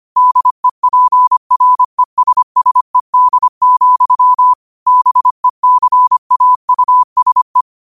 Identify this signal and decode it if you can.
Morse code, heard as a keyed tone, about 25 wpm, 1000 Hz. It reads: DEPRESSED, BECAUSE